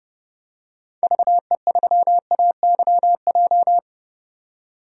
Morse code, heard as a keyed tone, about 31 wpm, 700 Hz. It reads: VE3AYJ
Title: DXCC-100 Call Signs - 6 Characters - 30wpm